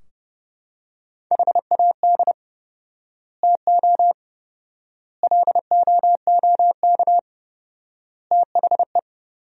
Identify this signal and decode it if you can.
Morse code, heard as a keyed tone, about 30 wpm, 700 Hz. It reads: HAD TO LOOK THE
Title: 4x Word-Spacing - Sentences from Top 100 Words - 30wpm